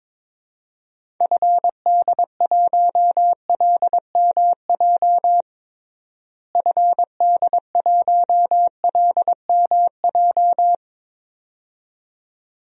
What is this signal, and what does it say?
Morse code, heard as a keyed tone, about 22 wpm, 700 Hz. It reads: FD1LMJ FD1LMJ
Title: DXCC-100 Call Signs - 6 Characters - Repeated 2x - 22wpm